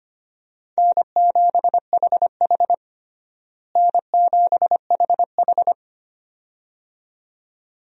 Morse code, heard as a keyed tone, about 25 wpm, 700 Hz. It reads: N7HH N7HH
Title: USA Call Signs - 4 Characters - Repeated 2x - 25wpm